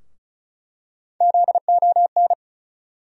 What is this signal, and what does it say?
Morse code, heard as a keyed tone, about 36 wpm, 700 Hz. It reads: ZON
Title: USA Call Sign Postfixes - 3 Letters 35wpm